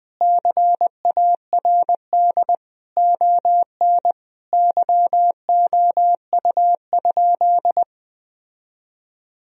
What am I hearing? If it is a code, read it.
Morse code, heard as a keyed tone, about 21 wpm, 700 Hz. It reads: CARD ON YOU?
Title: Sentences from Top 1000 Words - Review Part 4 - 20wpm